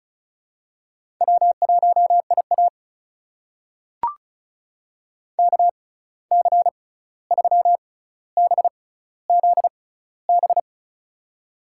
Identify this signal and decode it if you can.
Morse code, heard as a keyed tone, about 36 wpm, spaced as for 16 wpm, 700 Hz. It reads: W1IA KC3BZB
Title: USA Call Signs - 6x Character Spacing - 35wpm